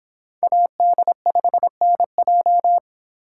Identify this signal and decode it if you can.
Morse code, heard as a keyed tone, about 26 wpm, 700 Hz. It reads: AD5NJ